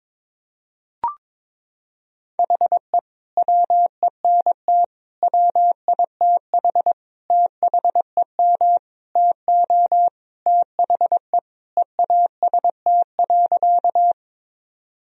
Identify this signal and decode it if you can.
Morse code, heard as a keyed tone, about 22 wpm, 690 Hz. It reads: HE WENT WITH THEM TO THE EAST.
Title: Sentences from Top 1000 Words - Review Part 3 - 22wpm